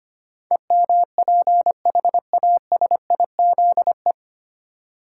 Morse code, heard as a keyed tone, about 25 wpm, 700 Hz. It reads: EMPHASIZE
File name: practice / 25wpm / 9 Letter Words - First 3 Letters Given - 25wpm